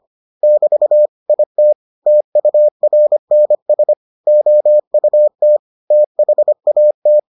Morse code, heard as a keyed tone, about 25 wpm, 600 Hz. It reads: <BT> IT TURNS OUT THAT